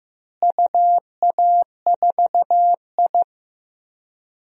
Morse code, heard as a keyed tone, about 15 wpm, 700 Hz. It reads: UA4I
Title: DXCC-100 Call Signs - 4 Characters - Repeated 3x - 15wpm